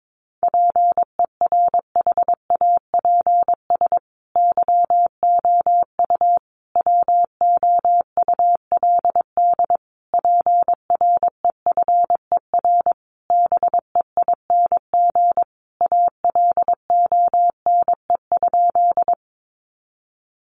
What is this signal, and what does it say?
Morse code, heard as a keyed tone, about 22 wpm, 700 Hz. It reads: PERHAPS YOU WOULD PREFER BEING ALONE?
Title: Sentences from Top 1100 Words 22wpm